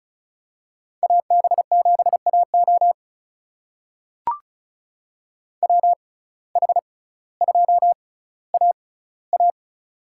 Morse code, heard as a keyed tone, about 34 wpm, spaced as for 18 wpm, 700 Hz. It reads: AB7AO WH2AA
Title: USA Call Signs - 5 Characters - 6x Character Spacing - 35wpm